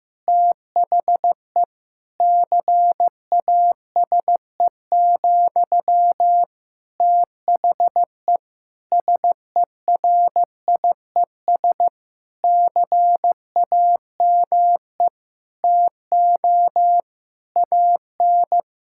Morse code, heard as a keyed tone, about 15 wpm, 700 Hz. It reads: THE CASE, THE SERIES CAME TO AN
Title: Sentences from Top 1000 Words - Review Part 5 - 15wpm